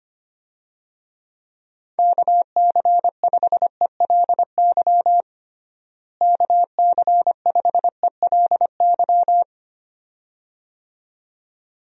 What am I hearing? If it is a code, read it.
Morse code, heard as a keyed tone, about 25 wpm, 700 Hz. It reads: KC5ELY KC5ELY
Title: USA Call Signs - 6 Characters - Repeated 2x - 25wpm